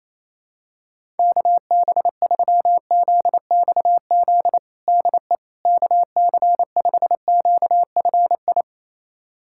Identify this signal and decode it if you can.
Morse code, heard as a keyed tone, about 28 wpm, 700 Hz. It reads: KB3ZXZ DE KC5QFI
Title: USA Callsign DE Callsign - 6 Characters - 28wpm